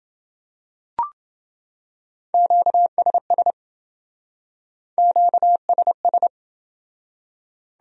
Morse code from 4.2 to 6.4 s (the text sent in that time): QSS